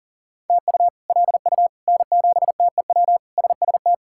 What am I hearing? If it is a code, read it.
Morse code, heard as a keyed tone, about 37 wpm, 700 Hz. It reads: TU LU N7TEW SST
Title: K1USN SST QSO Practice - 40wpm